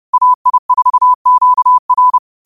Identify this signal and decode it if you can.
Morse code, heard as a keyed tone, about 30 wpm, 1000 Hz. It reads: AI4QR
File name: qotdfiles / 30wpm